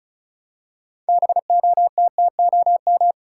Morse code, heard as a keyed tone, about 36 wpm, 700 Hz. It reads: BOTTOM
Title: Six Letter Words - 35wpm